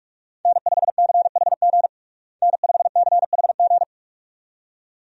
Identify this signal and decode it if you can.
Morse code, heard as a keyed tone, about 46 wpm, 700 Hz. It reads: N5CHG N5CHG
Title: USA Call Signs - 5 Characters - Repeated 2x - 45wpm